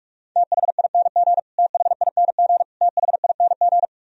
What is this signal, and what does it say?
Morse code, heard as a keyed tone, about 46 wpm, 700 Hz. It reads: THING THING THING
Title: Top 100 Words Repeated 45wpm